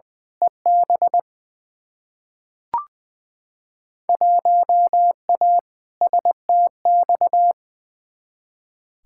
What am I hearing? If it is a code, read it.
Morse code, heard as a keyed tone, about 21 wpm, 700 Hz. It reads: EB 1A STX
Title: Field Day Exchanges Encore - 20wpm